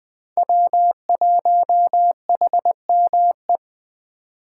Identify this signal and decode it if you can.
Morse code, heard as a keyed tone, about 20 wpm, 700 Hz. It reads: W1HME
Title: Speed Racing - USA 1x3 Call Signs 17wpm